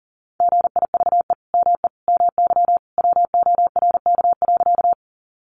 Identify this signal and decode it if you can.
Morse code, heard as a keyed tone, about 40 wpm, 700 Hz. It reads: GIVE ME MY WORK.
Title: Sentences from Top 200 Words 40wpm